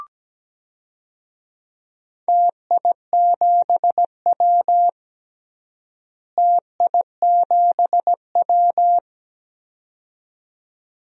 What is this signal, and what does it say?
Morse code, heard as a keyed tone, about 17 wpm, 700 Hz. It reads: TI7W TI7W
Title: DXCC-100 Call Signs - 4 Characters - Repeated 2x - 17wpm